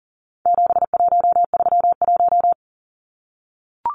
Morse code, heard as a keyed tone, about 40 wpm, 700 Hz. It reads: 7131
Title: Numbers - 4-Digit 40wpm